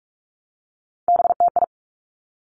Morse code, heard as a keyed tone, about 46 wpm, 700 Hz. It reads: BTI
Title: Three Challenging Letter-Number Combinations 45wpm